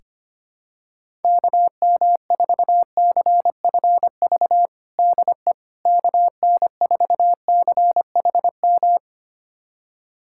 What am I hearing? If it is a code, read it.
Morse code, heard as a keyed tone, about 25 wpm, 700 Hz. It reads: KM4CFV DE KN4CHM
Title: USA Callsign DE Callsign - 6 Characters - 25wpm